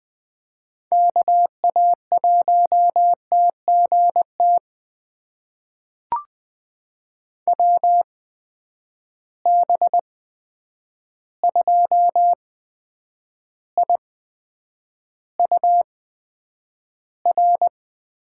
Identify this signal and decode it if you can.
Morse code, heard as a keyed tone, about 20 wpm, spaced as for 9 wpm, 700 Hz. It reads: KA1TGT WB2IUR
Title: USA Call Signs - 8x Character Spacing - 20wpm